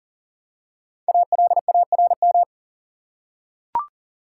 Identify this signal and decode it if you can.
Morse code, heard as a keyed tone, about 37 wpm, 700 Hz. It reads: ALARM
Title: Five Letter Words - 40wpm